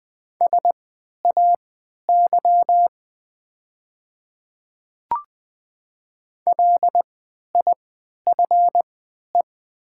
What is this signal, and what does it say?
Morse code, heard as a keyed tone, about 20 wpm, spaced as for 11 wpm, 700 Hz. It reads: SAY LIFE
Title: 3x Character Spacing - Top 300 Words - 20wpm